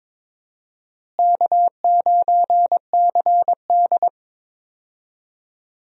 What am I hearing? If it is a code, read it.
Morse code, heard as a keyed tone, about 22 wpm, 700 Hz. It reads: K9CD
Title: USA Call Signs - 4 Characters - Repeated 2x - 22wpm